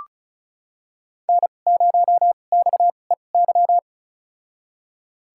Morse code, heard as a keyed tone, about 36 wpm, spaced as for 25 wpm, 700 Hz. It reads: N0XEY
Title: USA Call Signs - 5 Characters - 2x Character Spacing - 35wpm